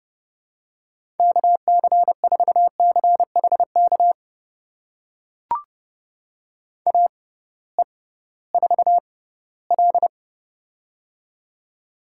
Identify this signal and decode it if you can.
Morse code, heard as a keyed tone, about 30 wpm, spaced as for 17 wpm, 700 Hz. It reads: KC4CHK AE4L
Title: USA Call Signs - 6x Character Spacing - 30wpm